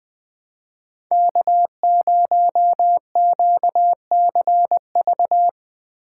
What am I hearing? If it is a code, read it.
Morse code, heard as a keyed tone, about 21 wpm, 700 Hz. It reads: K0QCV
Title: USA Call Signs - 5 Characters - 2x Character Spacing - 20wpm